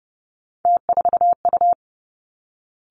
Morse code, heard as a keyed tone, about 30 wpm, 700 Hz. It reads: T4U
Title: Three Challenging Letter-Number Combinations 30wpm